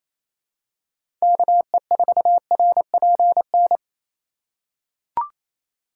Morse code, heard as a keyed tone, about 28 wpm, 700 Hz. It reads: KE4RPN
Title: USA Call Signs - 6x Character Spacing - 28wpm